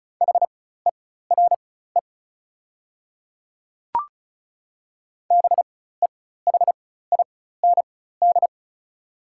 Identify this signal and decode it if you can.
Morse code, heard as a keyed tone, about 36 wpm, spaced as for 16 wpm, 690 Hz. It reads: HERE BEHIND
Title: 4x Character Spacing - Top 500 Words - 35wpm